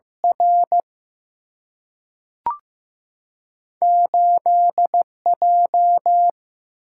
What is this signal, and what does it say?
Morse code, heard as a keyed tone, about 15 wpm, 700 Hz. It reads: R 8 J
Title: Single-Letter Single-Number 15wpm